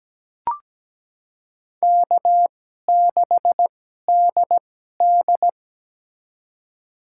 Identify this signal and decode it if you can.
Morse code, heard as a keyed tone, about 17 wpm, spaced as for 12 wpm, 700 Hz. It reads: K6DD
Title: USA Call Signs - 4 Characters - 2x Character Spacing - 17wpm